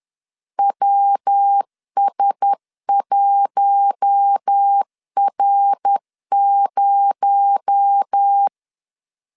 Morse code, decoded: WS1R0

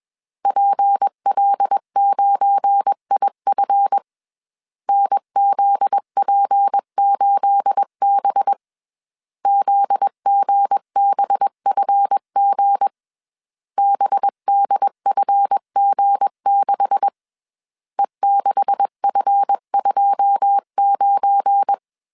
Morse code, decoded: PL9IF NZP8B ZGBFG BDFG6 E6F29

